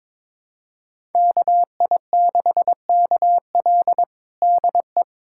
Morse code, heard as a keyed tone, about 22 wpm, 700 Hz. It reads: KI6KL DE
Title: USA Callsign DE Callsign 22wpm